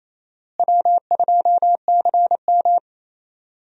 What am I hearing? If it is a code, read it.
Morse code, heard as a keyed tone, about 28 wpm, 700 Hz. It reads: W2CM
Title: USA Call Signs - 4 Characters - Repeated 2x - 28wpm